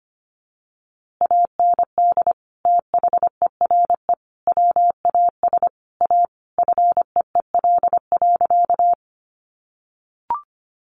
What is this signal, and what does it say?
Morse code, heard as a keyed tone, about 25 wpm, 700 Hz. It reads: AND THERE WAS A FEEL.
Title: A Sound Of Thunder 25wpm